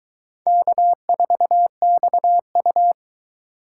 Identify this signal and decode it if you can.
Morse code, heard as a keyed tone, about 22 wpm, 700 Hz. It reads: K4XU